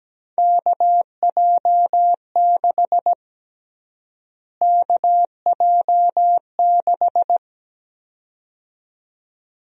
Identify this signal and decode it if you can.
Morse code, heard as a keyed tone, about 17 wpm, 700 Hz. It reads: KJ6 KJ6
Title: USA Call Sign Prefixes - Repeated 2x - 17wpm